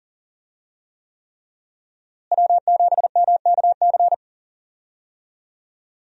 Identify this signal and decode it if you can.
Morse code, heard as a keyed tone, about 40 wpm, 700 Hz. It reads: W7MKC
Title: USA Call Signs - 5 Characters - Repeated 3x - 40wpm